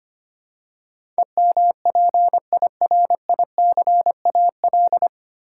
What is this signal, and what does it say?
Morse code, heard as a keyed tone, about 25 wpm, 700 Hz. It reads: EMPIRICAL
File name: Nine Letter Words - 25wpm